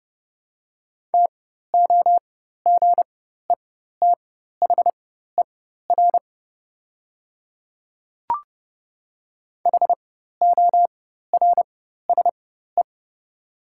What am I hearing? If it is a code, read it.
Morse code, heard as a keyed tone, about 30 wpm, spaced as for 14 wpm, 700 Hz. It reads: TOGETHER HORSE
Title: 4x Character Spacing - Top 500 Words - 30wpm